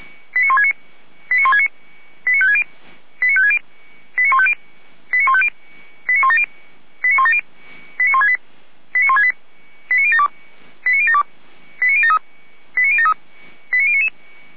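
Five-tone selective call (ZVEI), heard as quick runs of five tones, tones of about 70 ms. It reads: 89169 89169 89560 89560 89150 89150 89170 89170 89168 89072 89072 89073 89073 89099
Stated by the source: Five-tone ZVEI selective call received with an SDR, from the audio server of the Feuerwehr Orlamünde fire brigade.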